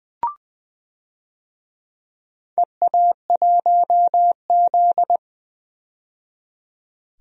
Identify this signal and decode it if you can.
Morse code, heard as a keyed tone, about 20 wpm, 700 Hz. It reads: EA1Z
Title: DXCC-100 Call Signs - 4 Characters - 20wpm